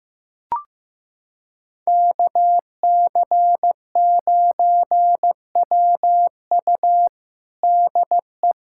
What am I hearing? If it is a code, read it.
Morse code, heard as a keyed tone, about 15 wpm, 700 Hz. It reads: KC9WU DE
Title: USA Callsign DE Callsign - 5 Characters - 15wpm